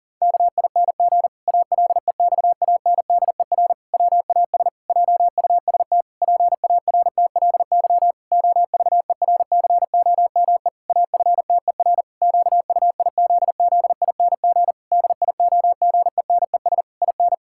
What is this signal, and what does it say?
Morse code, heard as a keyed tone, about 40 wpm, 700 Hz. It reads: KING ALEXANDER WAS JUST PARTLY OVERCOME AFTER QUIZZING DIOGENES IN
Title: Pangram Challenge v2 - 40wpm